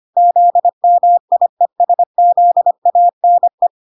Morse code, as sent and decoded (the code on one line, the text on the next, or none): --.. -- .. ... --.. .- -. .
ZMIESZANE